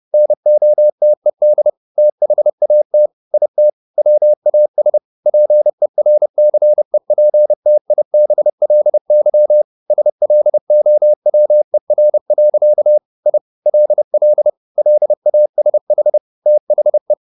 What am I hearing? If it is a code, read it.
Morse code, heard as a keyed tone, about 30 wpm, 600 Hz. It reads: NOTED THAT IT WAS PERCEPTIBLY SLOWER. I LL LASH THE